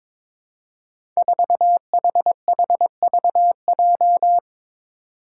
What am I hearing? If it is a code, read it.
Morse code, heard as a keyed tone, about 22 wpm, 700 Hz. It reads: 4HHVJ
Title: Five Challenging Letter-Number Combinations 22wpm